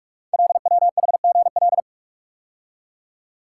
Morse code, heard as a keyed tone, about 44 wpm, 700 Hz. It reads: LWHGL
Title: Code Groups - 5 Letter:Numbers - 45wpm